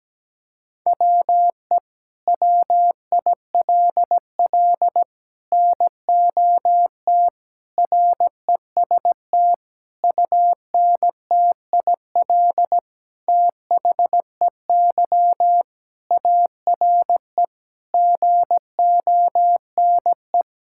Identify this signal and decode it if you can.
Morse code, heard as a keyed tone, about 17 wpm, 700 Hz. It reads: WE WILL NOT REST UNTIL THEY ARE GONE
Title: Sentences from Top 1600 Words 17wpm